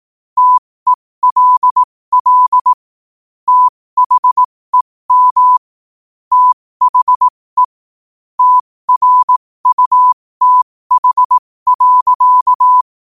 Morse code, heard as a keyed tone, about 18 wpm, spaced as for 15 wpm, 1000 Hz. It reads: TELL THEM THE TRUTH.
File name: qotdfiles / 15wpm